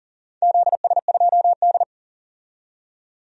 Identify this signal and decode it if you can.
Morse code, heard as a keyed tone, about 40 wpm, 700 Hz. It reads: ZS2D
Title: DXCC-100 Call Signs - 4 Characters - 40wpm